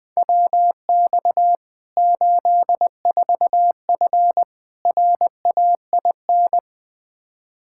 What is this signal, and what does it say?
Morse code, heard as a keyed tone, about 21 wpm, 700 Hz. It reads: WX 84F RAIN
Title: K1USN SST QSO Practice 20wpm